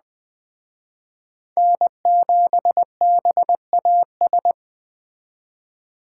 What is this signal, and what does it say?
Morse code, heard as a keyed tone, about 20 wpm, 700 Hz. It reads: N7BAS
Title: Speed Racing - USA 1x3 Call Signs 17wpm